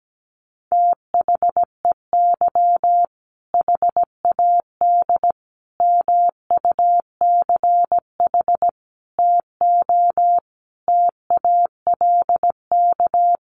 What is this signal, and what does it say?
Morse code, heard as a keyed tone, about 17 wpm, 700 Hz. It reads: THEY HAD MUCH TO TALK
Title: Sentences from Top 400 Words 17wpm